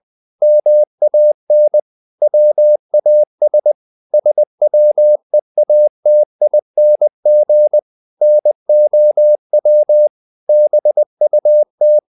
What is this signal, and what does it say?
Morse code, heard as a keyed tone, about 20 wpm, 600 Hz. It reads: MAN WAS SWEATING NOW BUT
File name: the old man and the sea part 2 20wpm